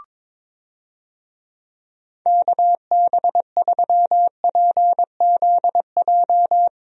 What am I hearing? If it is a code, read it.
Morse code, heard as a keyed tone, about 21 wpm, 700 Hz. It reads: KB3PZJ
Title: USA Call Signs - 6 Characters - Repeated 2x - 22wpm